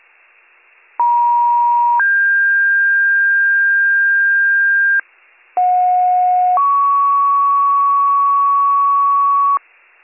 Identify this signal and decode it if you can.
Two-tone page, heard as a one-second tone, then a three-second tone, about 950 then 1600 Hz, about 730 then 1120 Hz, with noise. It is heard from 1.0 to 9.6 s.